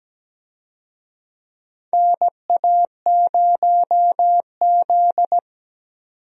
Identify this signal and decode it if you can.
Morse code, heard as a keyed tone, about 17 wpm, 700 Hz. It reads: NA0Z
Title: USA Call Signs - 4 Characters - 17wpm